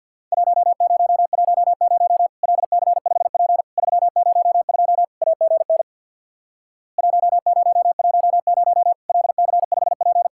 Morse code, heard as a keyed tone, about 50 wpm, 700 Hz. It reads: 1010 LC5P 302 AGN 1010 LC5P
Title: Worked All Europe DX Contest - QTC Exchange with Repeats - 50wpm